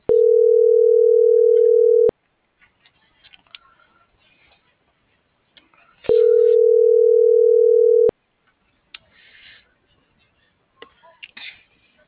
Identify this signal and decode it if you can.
Ringback tone (North America), 440 + 480 Hz.